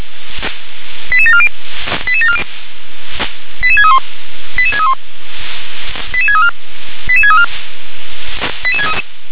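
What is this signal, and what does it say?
Five-tone selective call (ZVEI), heard as quick runs of five tones, tones of about 70 ms. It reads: 88740 88740 88631 88631 88634 88634
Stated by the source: ZVEI five-tone selective call 88730 received on SDR.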